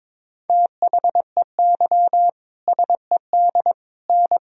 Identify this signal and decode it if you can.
Morse code, heard as a keyed tone, about 21 wpm, 700 Hz. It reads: THEY SED N
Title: Sentences from Top 100 Words and Abbreviations 22wpm